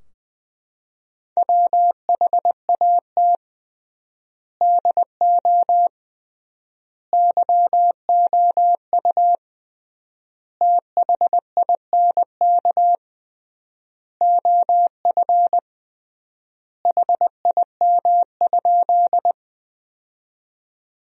Morse code, heard as a keyed tone, about 21 wpm, 700 Hz. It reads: WHAT DO YOU THINK OF HIM?
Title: 3x Word-Spacing - Sentences from Top 200 Words - 20wpm